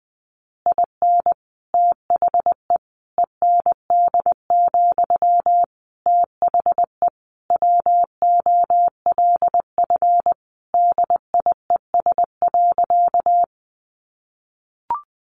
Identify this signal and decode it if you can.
Morse code, heard as a keyed tone, about 19 wpm, 700 Hz. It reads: IN THE END, THE WOLF DIES.
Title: Additional Easy Phrases 20wpm